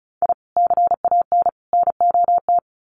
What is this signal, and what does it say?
Morse code, heard as a keyed tone, about 34 wpm, 700 Hz. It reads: I CAN NOT
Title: Sets of 3 Words 35wpm